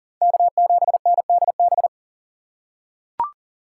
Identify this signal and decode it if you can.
Morse code, heard as a keyed tone, about 40 wpm, 700 Hz. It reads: K7NDB
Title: USA Call Signs - 5 Characters - 3x Character Spacing - 40wpm